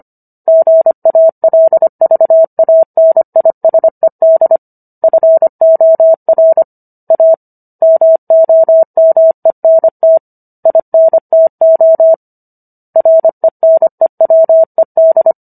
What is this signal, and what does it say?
Morse code, heard as a keyed tone, about 25 wpm, 650 Hz. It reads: GALVANISED FOR A MOMENT INTO RENEWED